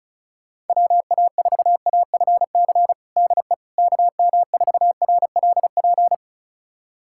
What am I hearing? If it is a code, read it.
Morse code, heard as a keyed tone, about 34 wpm, 700 Hz. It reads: WA4AFC DE KM4RLP
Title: USA Callsign DE Callsign - 6 Characters - 35wpm